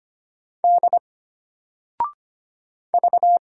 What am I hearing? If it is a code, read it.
Morse code, heard as a keyed tone, about 25 wpm, 700 Hz. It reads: D V
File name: Single-Letter-Number-25wpm